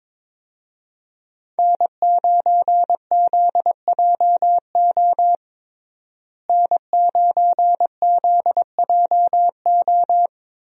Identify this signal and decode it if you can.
Morse code, heard as a keyed tone, about 22 wpm, 700 Hz. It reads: N9ZJO N9ZJO
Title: USA Call Signs - 5 Characters - Repeated 3x - 22wpm